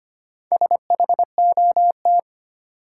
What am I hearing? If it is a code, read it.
Morse code, heard as a keyed tone, about 25 wpm, 700 Hz. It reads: SHOT
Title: Sentences from Top 1100 Words - New Word Drill Alt - 25wpm